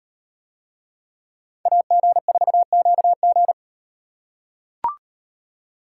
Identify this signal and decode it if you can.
Morse code, heard as a keyed tone, about 37 wpm, 700 Hz. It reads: AG4QG